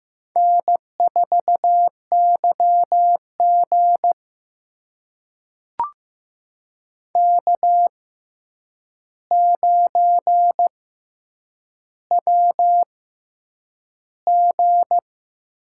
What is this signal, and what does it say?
Morse code, heard as a keyed tone, about 15 wpm, 700 Hz. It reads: N 4 Y G K 9 W G
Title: USA Call Signs - 4 Characters - 6x Character Spacing - 15wpm